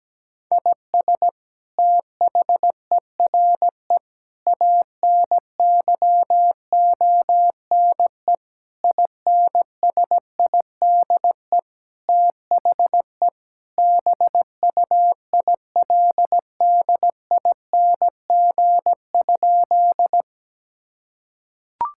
Morse code, heard as a keyed tone, about 17 wpm, 700 Hz. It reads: IS THERE ANYONE INSIDE THE BUILDING?
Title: Sentences from Top 1000 Words - Review Part 6 - 17wpm